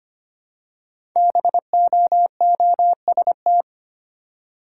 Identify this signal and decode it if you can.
Morse code, heard as a keyed tone, about 25 wpm, 700 Hz. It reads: BOOST
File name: 5 Letter Words - First 3 Letters Given - 25wpm